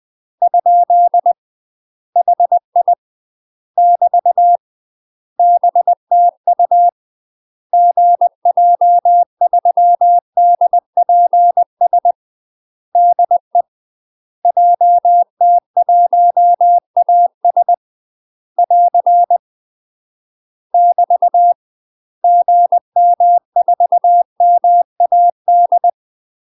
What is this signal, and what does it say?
Morse code, heard as a keyed tone, about 20 wpm, 700 Hz. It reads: ? HI <BT> BTU GJ3DPS DE JT1AS <AR> <BT> GM4MAD
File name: UK MASTER QSOs TRAINING / 20WPM / UK MASTER QSOs 0032